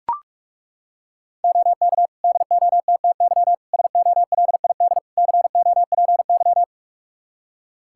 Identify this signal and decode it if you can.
Morse code, heard as a keyed tone, about 42 wpm, 700 Hz. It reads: OK DOTTY SOLID COPY
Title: K1USN SST QSO Practice 45wpm